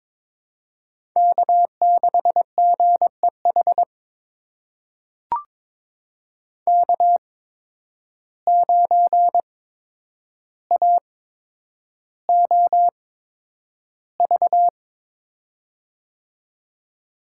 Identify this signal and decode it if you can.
Morse code, heard as a keyed tone, about 22 wpm, spaced as for 9 wpm, 700 Hz. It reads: K6GEH K9AOV
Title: USA Call Signs - 5 Characters - 8x Character Spacing - 22wpm